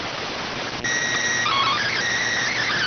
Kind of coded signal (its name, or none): SSTV (slow-scan television)